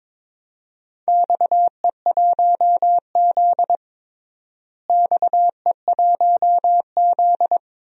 Morse code, heard as a keyed tone, about 22 wpm, 700 Hz. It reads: XE1Z XE1Z
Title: DXCC-100 Call Signs - 4 Characters - Repeated 3x - 22wpm